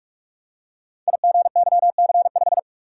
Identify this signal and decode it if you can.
Morse code, heard as a keyed tone, about 44 wpm, 700 Hz. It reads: IGYC5
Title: Code Groups - 5 Letter:Numbers - 45wpm